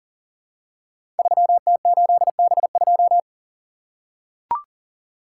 Morse code, heard as a keyed tone, about 38 wpm, 700 Hz. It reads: 3T8B2
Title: Code Groups - 5 Letter:Numbers - 40wpm